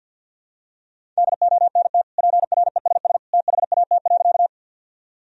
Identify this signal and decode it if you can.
Morse code, heard as a keyed tone, about 50 wpm, 700 Hz. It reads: DONT PRESS THAT.
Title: Sentences from Top 1000 Words - Review Part 1 - 50wpm